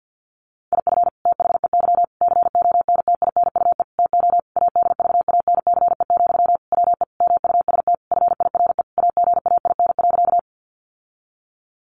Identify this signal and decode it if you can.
Morse code, heard as a keyed tone, about 50 wpm, 700 Hz. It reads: IF THEY CONTINUE TO ADVANCE, WE MUST FIRE AGAIN.
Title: Sentences from Top 1400 Words 50wpm